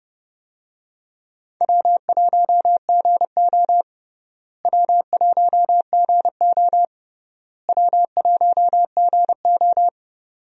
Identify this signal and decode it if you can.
Morse code, heard as a keyed tone, about 30 wpm, 700 Hz. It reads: W1GO W1GO W1GO
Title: USA Call Signs - 4 Characters - Repeated 3x - 30wpm